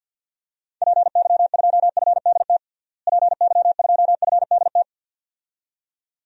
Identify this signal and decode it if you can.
Morse code, heard as a keyed tone, about 53 wpm, 700 Hz. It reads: PY2FDT PY2FDT
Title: DXCC-100 Call Signs - 6 Characters - Repeated 2x - 50wpm